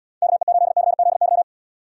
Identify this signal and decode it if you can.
Morse code, heard as a keyed tone, about 79 wpm, 700 Hz. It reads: KE8CZY